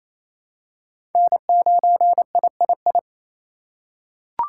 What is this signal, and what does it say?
Morse code, heard as a keyed tone, about 28 wpm, 700 Hz. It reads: N9III
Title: USA Call Signs - 2x Character Spacing - 28wpm